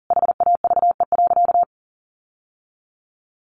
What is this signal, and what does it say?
Morse code, heard as a keyed tone, about 40 wpm, 700 Hz. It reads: HAVE.
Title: Sentences from Top 1400 Words 40wpm